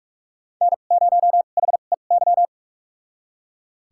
Morse code, heard as a keyed tone, about 42 wpm, spaced as for 31 wpm, 700 Hz. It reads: N0HEY
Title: USA Call Signs - 5 Characters - 2x Character Spacing - 45wpm